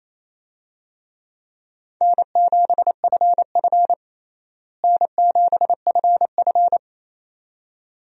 Morse code, heard as a keyed tone, about 28 wpm, 700 Hz. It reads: N7FF N7FF
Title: USA Call Signs - 4 Characters - Repeated 2x - 28wpm